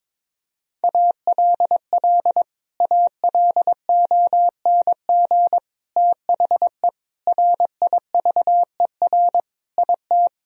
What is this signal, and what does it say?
Morse code, heard as a keyed tone, about 22 wpm, 700 Hz. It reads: ALL ALONG THE RIVER IT
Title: Sentences from Top 1000 Words - Review Part 2 - 22wpm